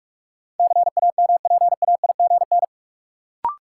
Morse code, heard as a keyed tone, about 46 wpm, 700 Hz. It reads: CAMPAIGN
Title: Sentences from Top 1200 Words - New Word Drill - 45wpm